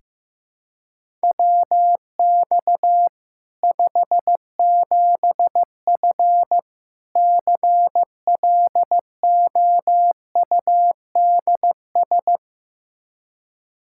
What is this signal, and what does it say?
Morse code, heard as a keyed tone, about 15 wpm, 700 Hz. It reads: WX 57F CLOUDS
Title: K1USN SST QSO Practice 15wpm